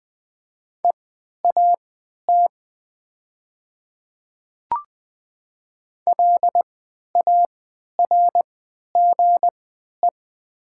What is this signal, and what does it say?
Morse code, heard as a keyed tone, about 20 wpm, spaced as for 11 wpm, 700 Hz. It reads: EAT LARGE
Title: 3x Character Spacing - Top 300 Words - 20wpm